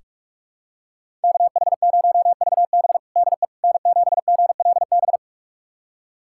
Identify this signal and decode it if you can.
Morse code, heard as a keyed tone, about 46 wpm, 700 Hz. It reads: KH0VB DE N7GLB
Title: USA Callsign DE Callsign - 5 Characters - 45wpm